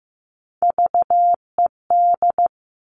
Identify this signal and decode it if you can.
Morse code, heard as a keyed tone, about 15 wpm, 700 Hz. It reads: VED
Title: All Cut Numbers - 3-Digit 15wpm